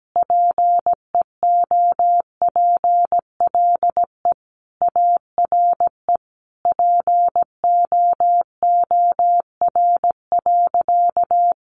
Morse code, heard as a keyed tone, about 17 wpm, 700 Hz. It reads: PEOPLE ARE POOR.
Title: Sentences from Top 1100 Words 17wpm